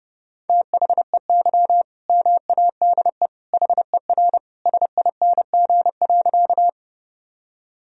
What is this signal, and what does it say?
Morse code, heard as a keyed tone, about 30 wpm, 700 Hz. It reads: THEY MADE HER SING.
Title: Sentences from Top 1000 Words - Review Part 2 - 30wpm